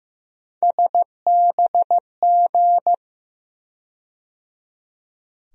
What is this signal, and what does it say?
Morse code, heard as a keyed tone, about 15 wpm, 700 Hz. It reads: SBG
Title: USA Call Sign Postfixes - 3 Letters 15wpm